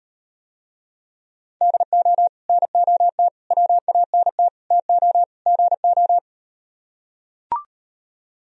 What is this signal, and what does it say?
Morse code, heard as a keyed tone, about 38 wpm, 700 Hz. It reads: DO NOT WANT TO GO